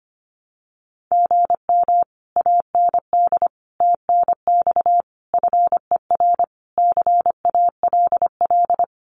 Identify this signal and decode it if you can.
Morse code, heard as a keyed tone, about 25 wpm, 700 Hz. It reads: GM AND TNX FER CALL
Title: Common QSO Phrases Encore 25wpm